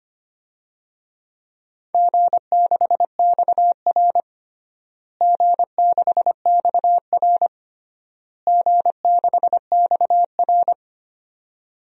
Morse code, heard as a keyed tone, about 25 wpm, 700 Hz. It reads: G6XR G6XR G6XR
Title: DXCC-100 Call Signs - 4 Characters - Repeated 3x - 25wpm